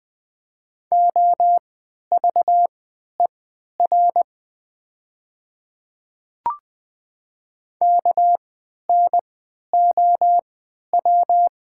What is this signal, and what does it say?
Morse code, heard as a keyed tone, about 20 wpm, spaced as for 11 wpm, 700 Hz. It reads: OVER KNOW
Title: 3x Character Spacing - Top 100 Words - 20wpm